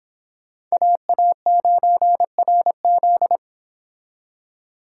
Morse code, heard as a keyed tone, about 30 wpm, 700 Hz. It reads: AA9RZ